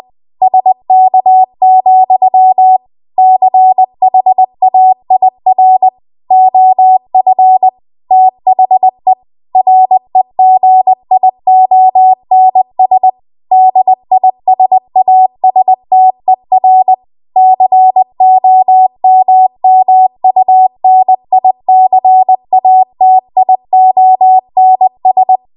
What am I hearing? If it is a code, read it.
Morse code, heard as a keyed tone, about 20 wpm, 750 Hz. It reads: SK, CHAIR OF THE REGIONS DISASTER COMMUNICATIONS
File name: Archive / 20 WPM / 20WPM